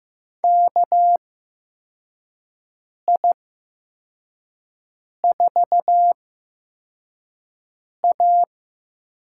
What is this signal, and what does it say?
Morse code, heard as a keyed tone, about 15 wpm, spaced as for 4 wpm, 700 Hz. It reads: KI4A
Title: USA Call Signs - 8x Character Spacing - 15wpm